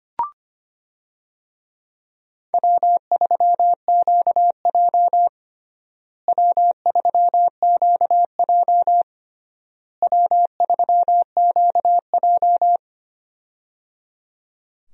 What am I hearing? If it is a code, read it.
Morse code, heard as a keyed tone, about 25 wpm, 700 Hz. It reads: W3QJ W3QJ W3QJ
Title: USA Call Signs - 4 Characters - Repeated 3x - 25wpm